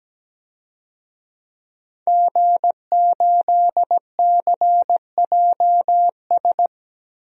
Morse code, heard as a keyed tone, about 17 wpm, 700 Hz. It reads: G8CJS
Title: DXCC-100 Call Signs - 5 Characters - Repeated 3x - 17wpm